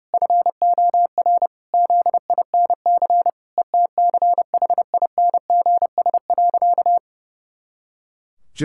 Morse code, heard as a keyed tone, about 32 wpm, 700 Hz. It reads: FOR ZINC ETCHINGS.
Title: Pangram Challenge v2 - 30wpm